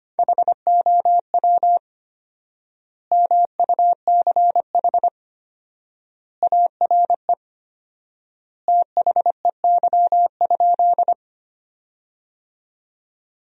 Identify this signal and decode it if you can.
Morse code, heard as a keyed tone, about 25 wpm, 700 Hz. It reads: HOW MUCH ARE THEY?
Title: 4x Word-Spacing - Sentences from Top 200 Words - 25wpm